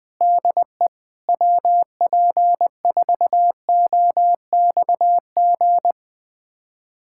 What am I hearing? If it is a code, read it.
Morse code, heard as a keyed tone, about 21 wpm, 700 Hz. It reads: DE WP4OXG
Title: USA Callsign DE Callsign - 6 Characters - 20wpm